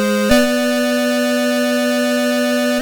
There is no code.